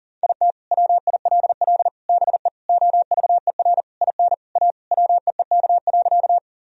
Morse code, heard as a keyed tone, about 37 wpm, 700 Hz. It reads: IT WILL BE OVER IN A WEEK.
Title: Sentences from Top 1000 Words - Review Part 2 - 40wpm